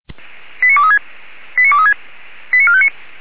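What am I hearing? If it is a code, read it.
Five-tone selective call (ZVEI), heard as quick runs of five tones, tones of about 70 ms. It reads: 89236 89236 89459
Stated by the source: Received by SDR; ZVEI five-tone code 89459.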